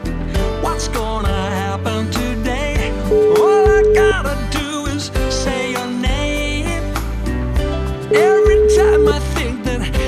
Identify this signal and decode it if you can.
Ringback tone (Europe, 425 Hz), with noise.